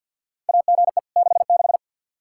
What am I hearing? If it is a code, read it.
Morse code, heard as a keyed tone, about 52 wpm, 700 Hz. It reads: AGE 66
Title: Practice QSO - Exchange 3 - More Info - 50wpm